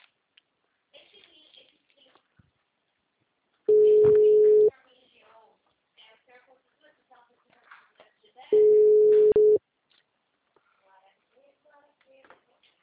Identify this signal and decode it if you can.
Ringback tone (Europe, 425 Hz).